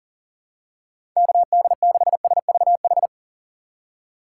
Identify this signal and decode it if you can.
Morse code, heard as a keyed tone, about 40 wpm, 700 Hz. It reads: KD6SVH
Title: USA Call Signs - 6 Characters - 6x Character Spacing - 40wpm